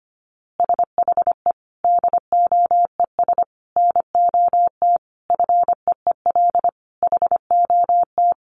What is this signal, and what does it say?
Morse code, heard as a keyed tone, about 25 wpm, 700 Hz. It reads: SHE DOES NOT FEEL HOT
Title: Easy Phrases 25wpm